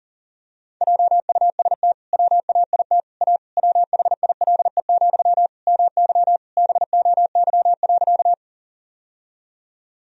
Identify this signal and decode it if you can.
Morse code, heard as a keyed tone, about 40 wpm, 700 Hz. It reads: JUST WAIT A WHILE, MY BOY.
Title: Sentences from Top 1000 Words - Review Part 1 - 40wpm